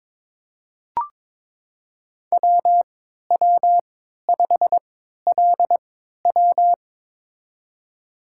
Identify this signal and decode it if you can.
Morse code, heard as a keyed tone, about 22 wpm, spaced as for 13 wpm, 700 Hz. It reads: WW5LW
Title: USA Call Signs - 3x Character Spacing - 22wpm